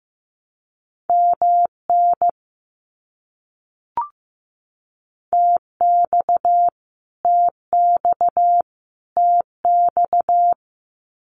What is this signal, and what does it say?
Morse code, heard as a keyed tone, about 15 wpm, 700 Hz. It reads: MN TX TX TX
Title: US States Repeated 3x 15wpm